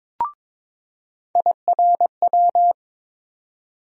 Morse code, heard as a keyed tone, about 22 wpm, 700 Hz. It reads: IRW
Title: USA Call Sign Postfixes - 3 Letters 22wpm